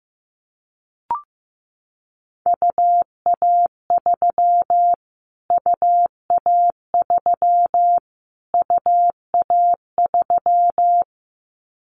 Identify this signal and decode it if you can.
Morse code, heard as a keyed tone, about 15 wpm, 700 Hz. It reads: UA3 UA3 UA3
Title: Callsign Prefixes for Top 25 DXCC - Repeated 3x - 15wpm